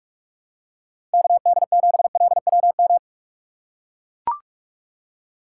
Morse code, heard as a keyed tone, about 44 wpm, 700 Hz. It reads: KD7LWM